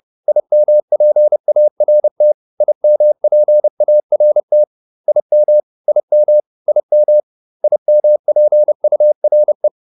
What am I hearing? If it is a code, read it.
Morse code, heard as a keyed tone, about 30 wpm, 600 Hz. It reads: IMPART IMPART IM IM IM IMPURE
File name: IM6 30